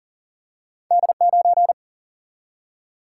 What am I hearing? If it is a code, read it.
Morse code, heard as a keyed tone, about 38 wpm, 700 Hz. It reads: D9
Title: Single-Letter Single-Number 40wpm